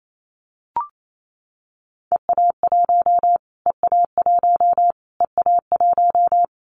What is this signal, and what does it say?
Morse code, heard as a keyed tone, about 28 wpm, 700 Hz. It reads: EA1 EA1 EA1
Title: Callsign Prefixes for Top 25 DXCC - Repeated 3x - 28wpm